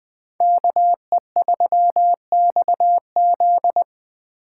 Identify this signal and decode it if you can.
Morse code, heard as a keyed tone, about 23 wpm, 700 Hz. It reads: KE3XZ
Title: Speed Racing - USA 2x2 Call Signs 20wpm